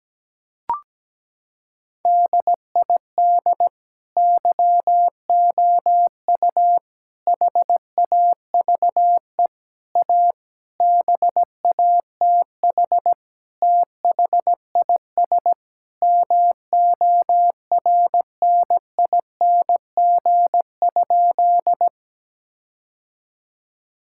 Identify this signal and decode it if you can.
Morse code, heard as a keyed tone, about 17 wpm, 700 Hz. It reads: DID YOU HAVE A BATH THIS MORNING?
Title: Sentences from Top 1600 Words 17wpm